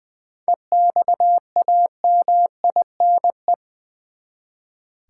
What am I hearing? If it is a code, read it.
Morse code, heard as a keyed tone, about 20 wpm, 700 Hz. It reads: EXAMINE
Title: Seven Letter Words - 20wpm